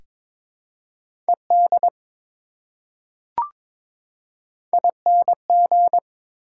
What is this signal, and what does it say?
Morse code, heard as a keyed tone, about 22 wpm, 700 Hz. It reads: ED ING
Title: Prefixes and Suffixes 22wpm